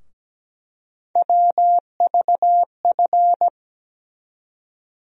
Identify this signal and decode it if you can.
Morse code, heard as a keyed tone, about 17 wpm, 700 Hz. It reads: WVF
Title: USA Call Sign Postfixes - 3 Letters 17wpm